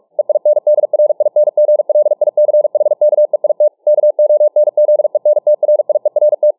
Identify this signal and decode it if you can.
Morse code, heard as a keyed tone, about 46 wpm, 600 Hz. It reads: EINDRINGLICHKEIT KONZENTRIERT